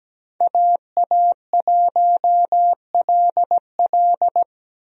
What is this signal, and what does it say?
Morse code, heard as a keyed tone, about 17 wpm, 700 Hz. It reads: AA1LL
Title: USA Call Signs - 5 Characters - 17wpm